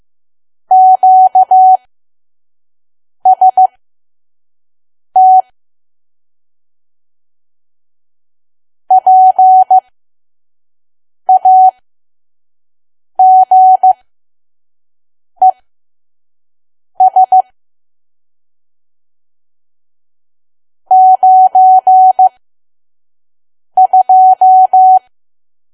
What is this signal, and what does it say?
Morse code, heard as a keyed tone, about 15 wpm, spaced as for 5 wpm, 750 Hz. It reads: QST PAGES 92
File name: Archive / 5 WPM files / PM